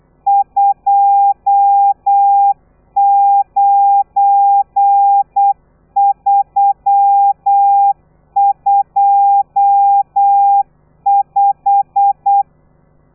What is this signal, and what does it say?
Morse code, heard as a keyed tone, about 8 wpm, 800 Hz. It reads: 29325
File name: telegrafia / i2tfj / Lez12 35